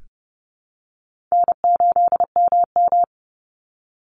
Morse code, heard as a keyed tone, about 30 wpm, 700 Hz. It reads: N8MM
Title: USA Call Signs 30wpm